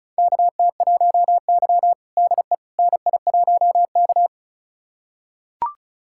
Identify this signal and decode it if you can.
Morse code, heard as a keyed tone, about 36 wpm, 700 Hz. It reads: KT1Y DE NI1K
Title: USA Callsign DE Callsign - 4 Characters - 35wpm